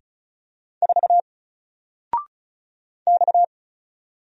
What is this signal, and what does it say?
Morse code, heard as a keyed tone, about 36 wpm, 700 Hz. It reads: 4 X
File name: Single-Letter-Number-35wpm